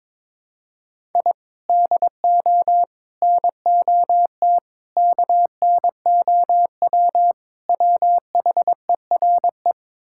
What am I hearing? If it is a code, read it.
Morse code, heard as a keyed tone, about 22 wpm, 700 Hz. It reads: I DO NOT KNOW WHERE
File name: practice / 22wpm / Build A Phrase - 5-Word Phrases - 22wpm